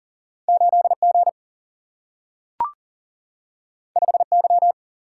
Morse code, heard as a keyed tone, about 42 wpm, 700 Hz. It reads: 8 G 5 Y
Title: Single-Letter Single-Number 40wpm